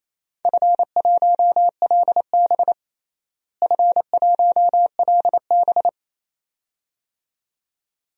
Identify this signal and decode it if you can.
Morse code, heard as a keyed tone, about 28 wpm, 700 Hz. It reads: F1LB F1LB
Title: DXCC-100 Call Signs - 4 Characters - Repeated 2x - 28wpm